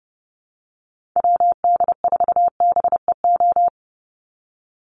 Morse code, heard as a keyed tone, about 30 wpm, 700 Hz. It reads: WD4BEO
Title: USA Call Signs 30wpm